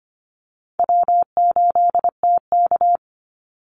Morse code, heard as a keyed tone, about 25 wpm, 700 Hz. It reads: W8TK
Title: Call Signs Repeated 3x - 25wpm